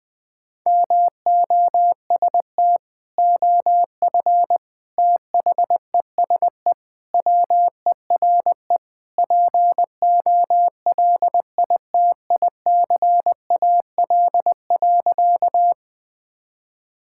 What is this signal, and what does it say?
Morse code, heard as a keyed tone, about 20 wpm, 700 Hz. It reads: MOST OF THESE WERE POLITICAL.
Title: Sentences from Top 1000 Words - Review Part 5 - 20wpm